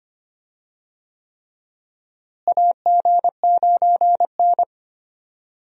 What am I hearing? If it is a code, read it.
Morse code, heard as a keyed tone, about 25 wpm, 700 Hz. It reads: AG9N
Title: USA Call Signs - 4 Characters - Repeated 3x - 25wpm